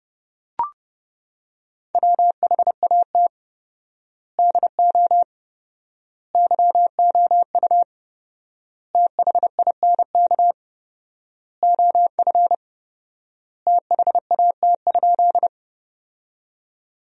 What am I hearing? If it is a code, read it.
Morse code, heard as a keyed tone, about 30 wpm, 700 Hz. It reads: WHAT DO YOU THINK OF THAT?
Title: 4x Word-Spacing - Sentences from Top 200 Words - 30wpm